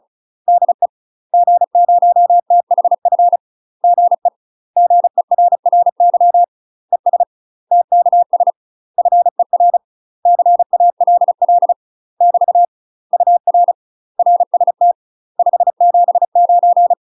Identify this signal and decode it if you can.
Morse code, heard as a keyed tone, about 35 wpm, 700 Hz. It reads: DE G0THF GE GERRY ES TKS FER CALL <BT> UR RST 579
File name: UK MASTER QSOs TRAINING / 35WPM / UK MASTER QSOs 0041